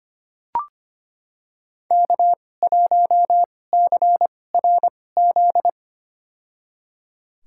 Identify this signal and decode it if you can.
Morse code, heard as a keyed tone, about 25 wpm, spaced as for 18 wpm, 700 Hz. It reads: K1CRZ
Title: USA Call Signs - 5 Characters - 2x Character Spacing - 25wpm